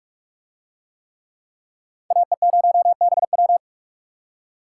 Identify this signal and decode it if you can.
Morse code, heard as a keyed tone, about 44 wpm, 700 Hz. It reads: AE0BW